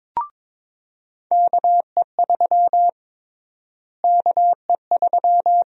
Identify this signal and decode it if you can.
Morse code, heard as a keyed tone, about 22 wpm, 700 Hz. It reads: KE3 KE3
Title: USA Call Sign Prefixes - Repeated 2x - 22wpm